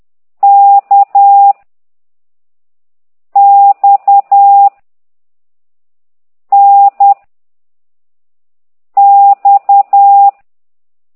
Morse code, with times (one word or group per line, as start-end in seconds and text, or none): 0.4-1.5 K
3.4-4.7 X
6.5-7.1 N
9.0-10.3 X